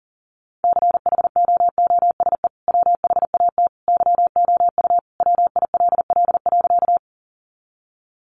Morse code, heard as a keyed tone, about 40 wpm, 700 Hz. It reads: CHOOSE WHAT YOU WILL.
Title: Sentences from Top 700 Words 40wpm